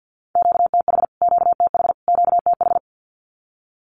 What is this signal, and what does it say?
Morse code, heard as a keyed tone, about 50 wpm, 700 Hz. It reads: QTH QTH QTH
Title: Q-Codes Repeated 3x 50wpm